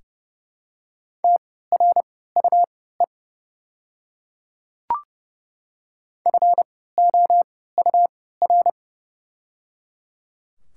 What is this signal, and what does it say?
Morse code, heard as a keyed tone, about 30 wpm, spaced as for 17 wpm, 700 Hz. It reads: TRUE FOUR
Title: 3x Character Spacing - Top 500 Words - 30wpm